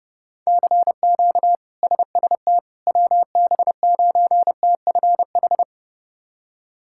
Morse code, heard as a keyed tone, about 30 wpm, 700 Hz. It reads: CQ SST WB9TFH
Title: K1USN SST QSO Practice - 30wpm